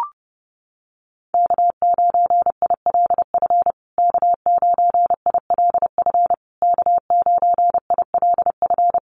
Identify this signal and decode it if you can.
Morse code, heard as a keyed tone, about 30 wpm, 700 Hz. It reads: K9ILF K9ILF K9ILF
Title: USA Call Signs Repeated 3x 30wpm